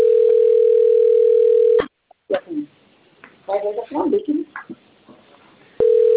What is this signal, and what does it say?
Ringback tone (North America), 440 + 480 Hz, with speech.
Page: Bebedouro 3